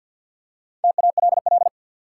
Morse code, heard as a keyed tone, about 53 wpm, 700 Hz. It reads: TALL
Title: Four Letter Words - 50wpm